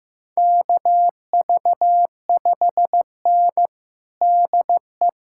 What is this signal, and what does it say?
Morse code, heard as a keyed tone, about 15 wpm, 700 Hz. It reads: KV5N DE
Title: USA Callsign DE Callsign - 4 Characters - 15wpm